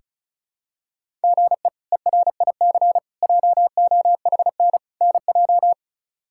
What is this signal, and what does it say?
Morse code, heard as a keyed tone, about 34 wpm, 700 Hz. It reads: GE ERIC JOHN NJ
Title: K1USN SST QSO Practice - 35wpm